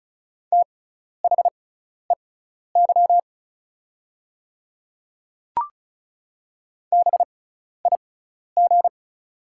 Morse code, heard as a keyed tone, about 36 wpm, spaced as for 12 wpm, 700 Hz. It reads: THEY BIG
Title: 6x Character Spacing - Top 200 Words - 35wpm